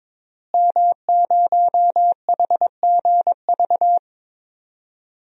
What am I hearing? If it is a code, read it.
Morse code, heard as a keyed tone, about 22 wpm, 700 Hz. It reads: M0HGV
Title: DXCC-100 Call Signs - 5 Characters - Repeated 3x - 22wpm